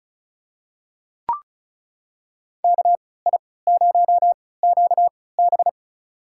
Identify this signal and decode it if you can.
Morse code, heard as a keyed tone, about 36 wpm, spaced as for 20 wpm, 700 Hz. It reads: KI0QB
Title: USA Call Signs - 5 Characters - 3x Character Spacing - 35wpm